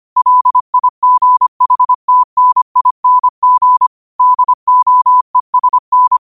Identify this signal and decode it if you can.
Morse code, heard as a keyed tone, about 25 wpm, 1000 Hz. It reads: LIGHTNING DOESN